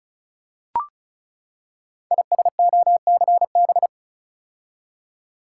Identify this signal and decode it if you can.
Morse code, heard as a keyed tone, about 36 wpm, 690 Hz. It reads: ISOCB
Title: Code Groups - 5 Letters - 35wpm